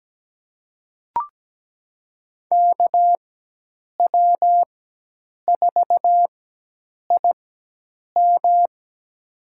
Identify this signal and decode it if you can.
Morse code, heard as a keyed tone, about 17 wpm, spaced as for 8 wpm, 700 Hz. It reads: KW4IM
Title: USA Call Signs - 5 Characters - 4x Character Spacing - 17wpm